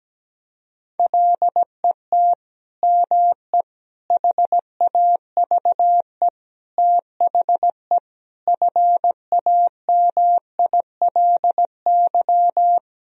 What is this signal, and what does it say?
Morse code, heard as a keyed tone, about 17 wpm, 700 Hz. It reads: LET ME HAVE THE FAMILY
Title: Sentences from Top 1000 Words - Review Part 2 - 17wpm